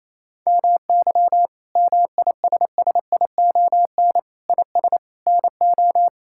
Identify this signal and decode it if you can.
Morse code, heard as a keyed tone, about 28 wpm, 700 Hz. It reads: MY MISSION IS NO
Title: Sentences from Top 1600 Words 28wpm